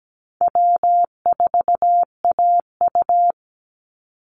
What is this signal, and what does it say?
Morse code, heard as a keyed tone, about 17 wpm, 700 Hz. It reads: W4AU
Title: Easy Call Signs - 17wpm